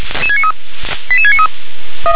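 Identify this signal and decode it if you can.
Five-tone selective call (ZVEI), heard as quick runs of five tones, tones of about 70 ms. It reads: 88693
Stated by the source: Five-tone ZVEI selective call received with an SDR.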